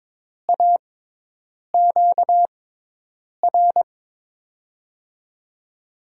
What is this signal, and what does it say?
Morse code, heard as a keyed tone, about 22 wpm, spaced as for 8 wpm, 700 Hz. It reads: AQR
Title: USA Call Signs - 5 Characters - 6x Character Spacing - 22wpm